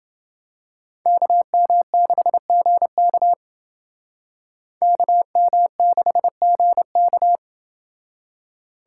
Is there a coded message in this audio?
Yes: Morse code from 1.1 to 7.4 s, heard as a keyed tone, about 28 wpm, 700 Hz. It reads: KM6GK KM6GK